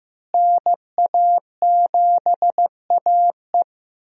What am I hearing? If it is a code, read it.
Morse code, heard as a keyed tone, about 15 wpm, 700 Hz. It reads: NA7AE
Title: USA Call Signs - 5 Characters - Repeated 2x - 15wpm